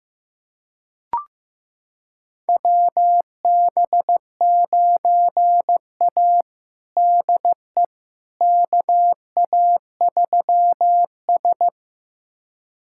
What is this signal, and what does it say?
Morse code, heard as a keyed tone, about 15 wpm, 700 Hz. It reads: WB9A DE KA3S
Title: USA Callsign DE Callsign - 4 Characters - 15wpm